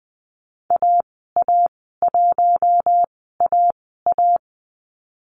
Morse code, heard as a keyed tone, about 20 wpm, spaced as for 14 wpm, 700 Hz. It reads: AA1AA
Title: 042 Lesson 8g - Number 1 - Call Signs 20wpm